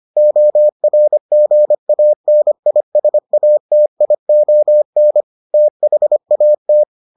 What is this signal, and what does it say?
Morse code, heard as a keyed tone, about 25 wpm, 600 Hz. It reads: ORGANISATION THAT